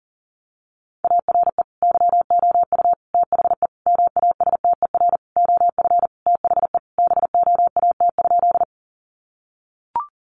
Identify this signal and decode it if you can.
Morse code, heard as a keyed tone, about 40 wpm, 700 Hz. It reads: ARE YOU THE MASTER OF THE BOAT?
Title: Sentences from Top 1300 Words 40wpm